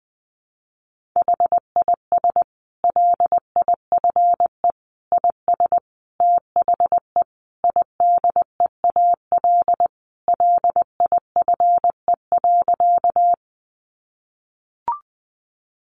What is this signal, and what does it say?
Morse code, heard as a keyed tone, about 20 wpm, 700 Hz. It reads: HIS LIFE IS THE IDEAL LIFE.
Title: Sentences from Top 1400 Words 20wpm